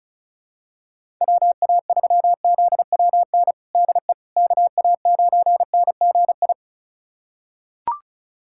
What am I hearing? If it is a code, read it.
Morse code, heard as a keyed tone, about 34 wpm, 700 Hz. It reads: WA3ZWN DE KA9NGI
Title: USA Callsign DE Callsign 35wpm